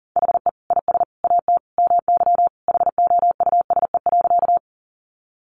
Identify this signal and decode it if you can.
Morse code, heard as a keyed tone, about 40 wpm, 700 Hz. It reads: HE IS AT MY HOUSE.
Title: Sentences from Top 200 Words 40wpm